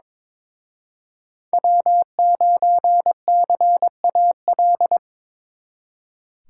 Morse code, heard as a keyed tone, about 26 wpm, 700 Hz. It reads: W9CAL
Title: Speed Racing - USA Call Signs 22wpm